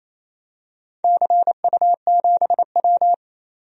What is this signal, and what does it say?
Morse code, heard as a keyed tone, about 28 wpm, 700 Hz. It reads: CU7W
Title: DXCC-100 Call Signs - 4 Characters - 28wpm